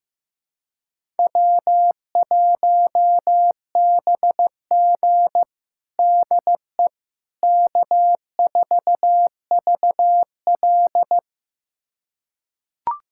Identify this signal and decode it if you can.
Morse code, heard as a keyed tone, about 15 wpm, 700 Hz. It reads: W1BG DE K4VL
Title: USA Callsign DE Callsign - 4 Characters - 15wpm